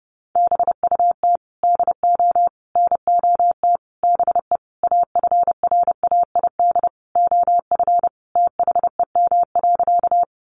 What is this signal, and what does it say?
Morse code, heard as a keyed tone, about 30 wpm, 700 Hz. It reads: BUT DO NOT BE AFRAID OF THEM.
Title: Sentences from Top 1500 Words 30wpm